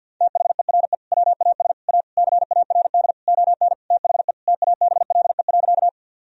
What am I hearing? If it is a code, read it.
Morse code, heard as a keyed tone, about 50 wpm, 700 Hz. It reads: THERE WAS A CARD ON THE TABLE.
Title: Sentences from Top 1000 Words - Review Part 4 - 50wpm